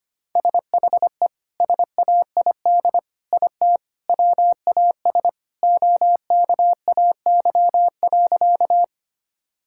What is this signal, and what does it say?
Morse code, heard as a keyed tone, about 25 wpm, 700 Hz. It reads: SHE SAID IT WAS OKAY.
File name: practice / 25wpm / Sentences with 2-Word Phrases - 25wpm